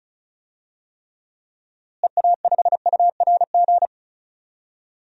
Morse code, heard as a keyed tone, about 36 wpm, 700 Hz. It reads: EA5URG
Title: DXCC-100 Call Signs - 6 Characters - Repeated 3x - 35wpm